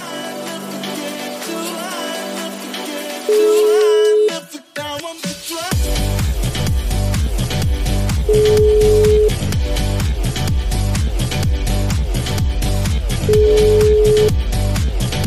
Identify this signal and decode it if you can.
Ringback tone (Europe, 425 Hz), with noise.